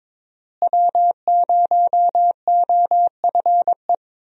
Morse code, heard as a keyed tone, about 21 wpm, 700 Hz. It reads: W0OFE
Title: USA Call Signs - 5 Characters - Repeated 3x - 22wpm